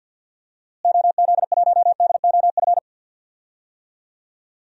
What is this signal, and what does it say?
Morse code, heard as a keyed tone, about 52 wpm, 700 Hz. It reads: OZ1DOF
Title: DXCC-100 Call Signs - 6 Characters - Repeated 2x - 50wpm